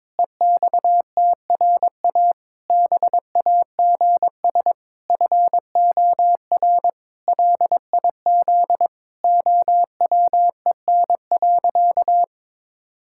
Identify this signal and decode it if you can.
Morse code, heard as a keyed tone, about 22 wpm, 700 Hz. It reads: EXTRA BAGS FOR LIZ OWEN.
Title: Pangram Challenge v2 - 22wpm